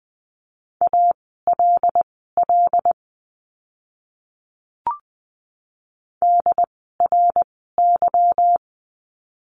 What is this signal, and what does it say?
Morse code, heard as a keyed tone, about 20 wpm, spaced as for 15 wpm, 700 Hz. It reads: ALL DRY
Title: 146 Lesson 24f - Letter y - Words 20wpm